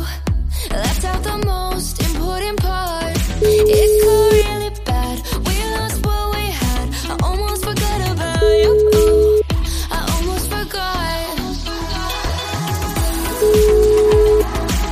Ringback tone (Europe, 425 Hz), with noise.